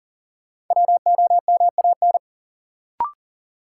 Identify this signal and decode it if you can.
Morse code, heard as a keyed tone, about 40 wpm, 700 Hz. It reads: WOMAN
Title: Sets of 5 Words - New Word Drill - 40wpm